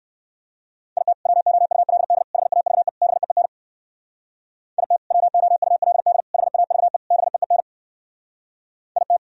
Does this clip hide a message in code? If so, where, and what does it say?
Yes: Morse code from 1.0 to 9.2 s, heard as a keyed tone, about 69 wpm, 700 Hz. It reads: IT WOULD HAVE BEEN IT WOULD HAVE BEEN IT